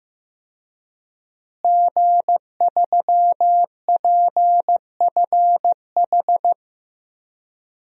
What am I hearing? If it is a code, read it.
Morse code, heard as a keyed tone, about 15 wpm, 700 Hz. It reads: G3PFH
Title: DXCC-100 Call Signs - 5 Characters - Repeated 3x - 15wpm